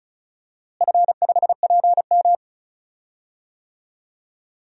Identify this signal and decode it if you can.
Morse code, heard as a keyed tone, about 34 wpm, 700 Hz. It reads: F5PM
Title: DXCC-100 Call Signs - 4 Characters - Repeated 3x - 35wpm